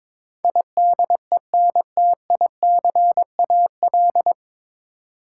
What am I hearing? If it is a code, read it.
Morse code, heard as a keyed tone, about 22 wpm, 700 Hz. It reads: IDENTICAL
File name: 9 Letter Words - First 3 Letters Given - 22wpm